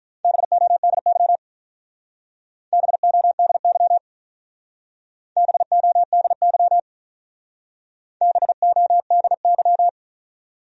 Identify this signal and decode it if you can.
Morse code, heard as a keyed tone, about 43 wpm, 700 Hz. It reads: BODY BODY BODY BODY
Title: Speed-Racing Top 500 Words 35wpm